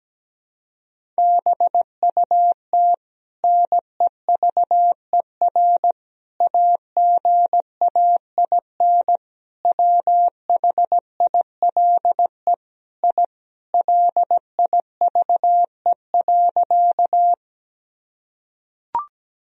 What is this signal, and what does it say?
Morse code, heard as a keyed tone, about 17 wpm, 700 Hz. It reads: BUT NEVER AGAIN WHILE I LIVE.
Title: Sentences from Top 1000 Words - Review Part 1 - 17wpm